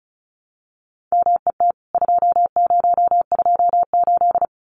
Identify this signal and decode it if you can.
Morse code, heard as a keyed tone, about 34 wpm, 700 Hz. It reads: MET 2028
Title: CWT Exchange 35wpm